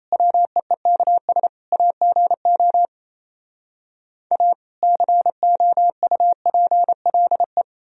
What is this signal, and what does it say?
Morse code, heard as a keyed tone, about 37 wpm, 700 Hz. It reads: WEEKS AGO A COUPLE